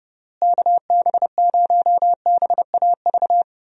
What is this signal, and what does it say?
Morse code, heard as a keyed tone, about 30 wpm, 700 Hz. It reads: KB0BAV
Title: USA Call Signs - 6 Characters - Repeated 2x - 30wpm